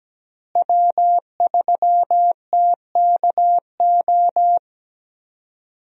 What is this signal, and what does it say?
Morse code, heard as a keyed tone, about 17 wpm, 700 Hz. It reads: W3TKO
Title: USA Call Signs - 5 Characters - Repeated 3x - 17wpm